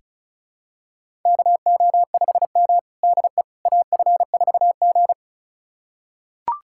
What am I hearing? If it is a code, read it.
Morse code, heard as a keyed tone, about 36 wpm, 700 Hz. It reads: KO5M DE AF4G
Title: USA Callsign DE Callsign - 4 Characters - 35wpm